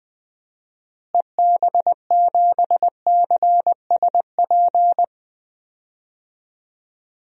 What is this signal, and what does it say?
Morse code, heard as a keyed tone, about 20 wpm, 700 Hz. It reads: EB7CSP
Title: DXCC-100 Call Signs - 6 Characters - Repeated 3x - 20wpm